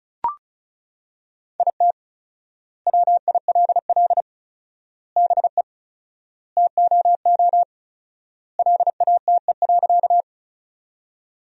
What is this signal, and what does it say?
Morse code, heard as a keyed tone, about 36 wpm, 700 Hz. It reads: IT WILL BE TOO LATE.
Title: 4x Word-Spacing - Sentences from Top 300 Words - 35wpm